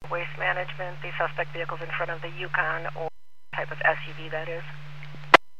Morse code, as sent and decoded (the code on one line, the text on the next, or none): none